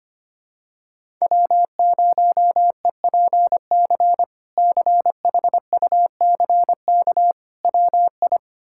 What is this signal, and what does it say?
Morse code, heard as a keyed tone, about 25 wpm, 700 Hz. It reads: W0EPC CHUCK WI
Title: K1USN SST QSO Practice - 25wpm